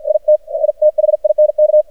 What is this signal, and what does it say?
Morse code, heard as a keyed tone, about 54 wpm, 610 Hz. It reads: IT I THINK
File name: archived / CRSnet